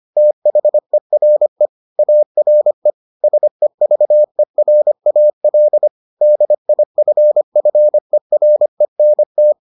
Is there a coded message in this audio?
Yes: Morse code from 0.2 to 9.5 s, heard as a keyed tone, about 25 wpm, 600 Hz. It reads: THERE ARE SEVERAL DIFFERENT